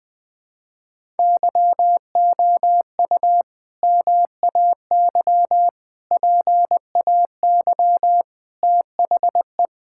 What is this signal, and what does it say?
Morse code, heard as a keyed tone, about 19 wpm, 700 Hz. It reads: YOU MAY PAY THE
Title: Sentences with 8-Word Phrases - Spoken Beginning - 20wpm